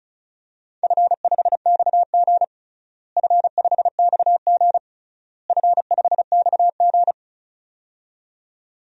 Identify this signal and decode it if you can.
Morse code, heard as a keyed tone, about 34 wpm, 700 Hz. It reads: F5XG F5XG F5XG
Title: DXCC-100 Call Signs - 4 Characters - Repeated 3x - 35wpm